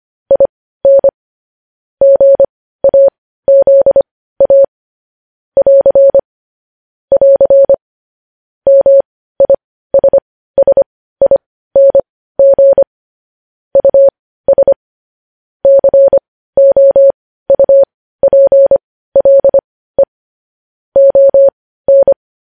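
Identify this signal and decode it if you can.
Morse code, heard as a keyed tone, about 25 wpm, spaced as for 15 wpm, 550 Hz. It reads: IN GAZA <AR> <AR> MISSING US COUPLE ON
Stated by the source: Morse code news headlines sent at 15 wpm.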